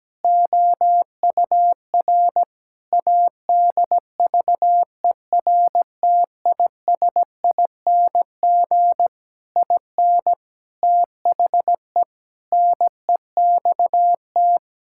Morse code, heard as a keyed tone, about 17 wpm, 700 Hz. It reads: OUR ADVERTISING IN THE NEXT
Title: Sentences from Top 1000 Words - Review Part 6 - 17wpm